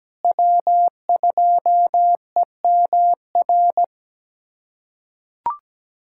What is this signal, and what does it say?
Morse code, heard as a keyed tone, about 17 wpm, 700 Hz. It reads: W2EMR
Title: USA Callsign DE Callsign - 5 Characters - 17wpm